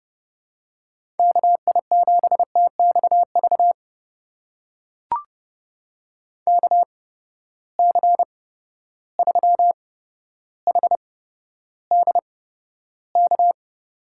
Morse code, heard as a keyed tone, about 29 wpm, spaced as for 13 wpm, 700 Hz. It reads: KI7TXV KC3HDK
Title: USA Call Signs - 6 Characters - 8x Character Spacing - 30wpm